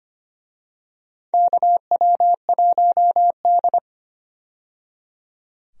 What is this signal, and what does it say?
Morse code, heard as a keyed tone, about 25 wpm, 700 Hz. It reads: KW1D
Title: USA Call Signs - 4 Characters - 25wpm